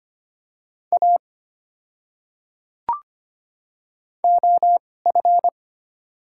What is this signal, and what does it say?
Morse code, heard as a keyed tone, about 25 wpm, spaced as for 18 wpm, 690 Hz. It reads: A OF
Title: 2x Character Spacing - Top 200 Words - 25wpm